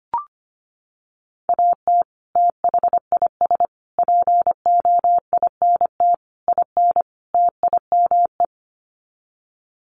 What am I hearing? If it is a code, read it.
Morse code, heard as a keyed tone, about 25 wpm, 700 Hz. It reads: AT THIS POINT IN TIME
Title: Sets of 5 Words Encore 25wpm